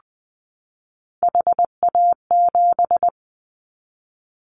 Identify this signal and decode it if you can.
Morse code, heard as a keyed tone, about 20 wpm, 700 Hz. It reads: HA7
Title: Callsign Prefixes for Top 25 DXCC - 20wpm